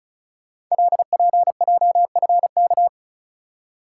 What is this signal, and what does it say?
Morse code, heard as a keyed tone, about 36 wpm, 700 Hz. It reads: LPJFK
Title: Code Groups - 5 Letters - 35wpm